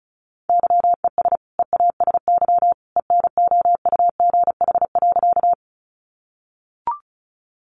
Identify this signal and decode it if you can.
Morse code, heard as a keyed tone, about 34 wpm, 700 Hz. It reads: YES EASY ENOUGH.
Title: Sentences from Top 700 Words 35wpm